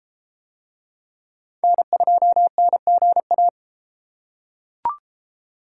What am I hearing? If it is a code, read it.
Morse code, heard as a keyed tone, about 32 wpm, 700 Hz. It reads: N2NGA